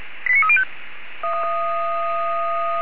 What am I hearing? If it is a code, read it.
Five-tone selective call (ZVEI), heard as quick runs of five tones, tones of about 80 ms. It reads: 89305